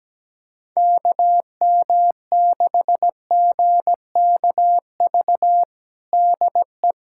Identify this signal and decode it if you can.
Morse code, heard as a keyed tone, about 17 wpm, 700 Hz. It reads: KM6GKV DE
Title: USA Callsign DE Callsign 17wpm